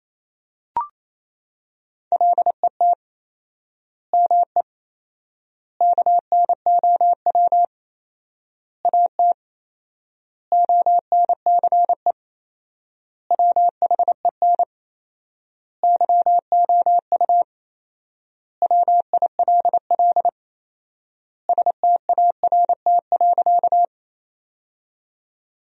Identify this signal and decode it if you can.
Morse code, heard as a keyed tone, about 28 wpm, 700 Hz. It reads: LET ME KNOW AT ONCE WHEN YOU WILL START.
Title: 4x Word-Spacing - Sentences from Top 300 Words - 28wpm